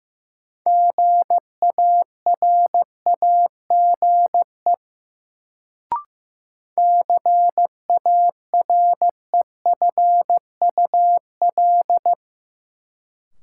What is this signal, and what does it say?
Morse code, heard as a keyed tone, about 15 wpm, 700 Hz. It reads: GARAGE CAREFUL
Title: Sentences from Top 1600 Words - New Word Drill Alt - 15wpm